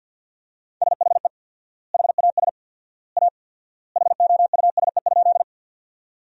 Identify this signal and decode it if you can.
Morse code, heard as a keyed tone, about 50 wpm, 700 Hz. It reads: SHE HAS A HOUSE?
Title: 4x Word-Spacing - Sentences from Top 200 Words - 50wpm